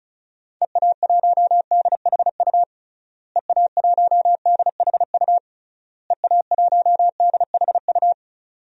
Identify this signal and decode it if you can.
Morse code, heard as a keyed tone, about 36 wpm, 700 Hz. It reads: EA1DHU EA1DHU EA1DHU
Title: DXCC-100 Call Signs - 6 Characters - Repeated 3x - 35wpm